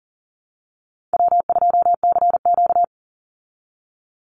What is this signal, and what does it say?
Morse code, heard as a keyed tone, about 40 wpm, 700 Hz. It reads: W2CQ
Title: Easy Call Signs Repeated 3x - 40wpm